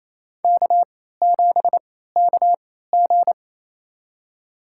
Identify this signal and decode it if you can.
Morse code, heard as a keyed tone, about 28 wpm, spaced as for 16 wpm, 700 Hz. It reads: K7KG
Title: USA Call Signs - 4 Characters - 3x Character Spacing - 28wpm